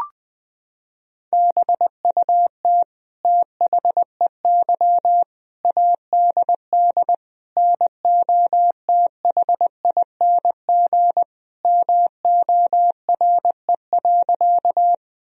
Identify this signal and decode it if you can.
Morse code, heard as a keyed tone, about 20 wpm, 700 Hz. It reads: BUT THEY ADD NOTHING MORE.
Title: Sentences from Top 1000 Words - Review Part 1 - 20wpm